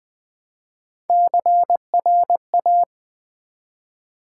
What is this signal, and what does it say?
Morse code, heard as a keyed tone, about 20 wpm, 700 Hz. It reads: CRA
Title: Build A Word - 1 Letter At A Time - 5-Letter Words - 20wpm